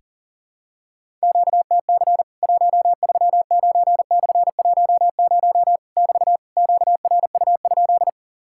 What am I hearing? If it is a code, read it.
Morse code, heard as a keyed tone, about 40 wpm, 700 Hz. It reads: QTC 139/10 <BT> QRU?
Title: Worked All Europe DX Contest - QTC Exchange - 40wpm